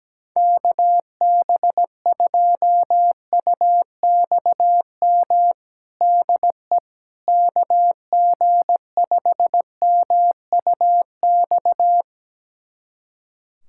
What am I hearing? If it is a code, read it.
Morse code, heard as a keyed tone, about 17 wpm, 700 Hz. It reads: KB2UXM DE KG5MUX
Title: USA Callsign DE Callsign - 6 Characters - 17wpm